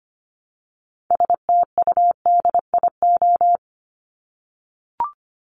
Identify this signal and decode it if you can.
Morse code, heard as a keyed tone, about 25 wpm, 700 Hz. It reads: STUDIO
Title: Top 1001-1250 Words 25wpm